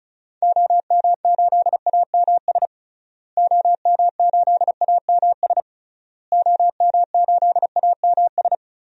Morse code, heard as a keyed tone, about 36 wpm, 700 Hz. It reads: OM8AMS OM8AMS OM8AMS
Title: DXCC-100 Call Signs - 6 Characters - Repeated 3x - 35wpm